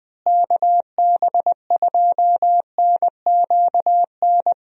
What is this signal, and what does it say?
Morse code, heard as a keyed tone, about 20 wpm, 700 Hz. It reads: KB2NQN
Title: USA Call Signs - 6 Characters - Repeated 2x - 20wpm